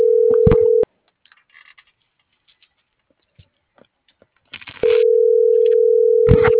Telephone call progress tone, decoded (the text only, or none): ringback tone